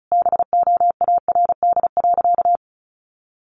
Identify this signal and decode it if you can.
Morse code, heard as a keyed tone, about 34 wpm, 700 Hz. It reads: BOARD.
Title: Sentences from Top 1100 Words 35wpm